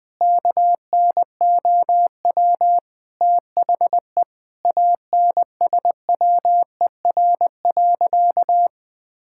Morse code, heard as a keyed tone, about 20 wpm, 700 Hz. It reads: KNOW THE ANSWER.
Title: Sentences with 4-Word Phrases - 20wpm